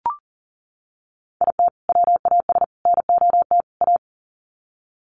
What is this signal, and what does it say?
Morse code, heard as a keyed tone, about 40 wpm, 700 Hz. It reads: IT WAS NOT A
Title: Sets of 4 Words 40wpm